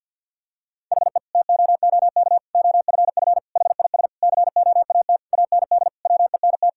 Morse code, heard as a keyed tone, about 50 wpm, 700 Hz. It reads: HE TOOK OFF HIS COAT AND WENT
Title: Sentences from Top 1600 Words 50wpm